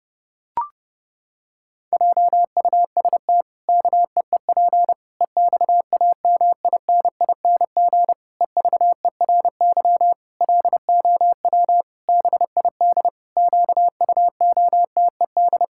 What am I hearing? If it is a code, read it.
Morse code, heard as a keyed tone, about 32 wpm, 700 Hz. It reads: JUST KEEP EXAMINING EVERY LOW BID QUOTED
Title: Pangram Challenge v2 - 30wpm